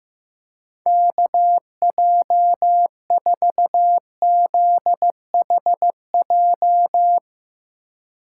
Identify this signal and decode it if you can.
Morse code, heard as a keyed tone, about 15 wpm, 700 Hz. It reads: KJ4ZHJ
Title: USA Call Signs - 6 Characters - 4x Character Spacing - 15wpm